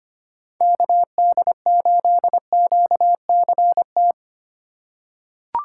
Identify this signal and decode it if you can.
Morse code, heard as a keyed tone, about 25 wpm, 700 Hz. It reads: KD8QCT
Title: USA Call Signs - 6 Characters - 3x Character Spacing - 25wpm